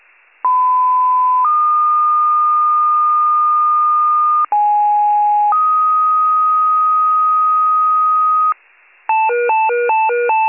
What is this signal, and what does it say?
Two-tone page, heard as a one-second tone, then a three-second tone, about 1010 then 1220 Hz, about 830 then 1250 Hz. It is heard from 0.4 to 8.5 s.